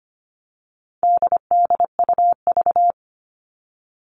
Morse code, heard as a keyed tone, about 25 wpm, 700 Hz. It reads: DDUV
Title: All Cut Numbers - 4-Digit 25wpm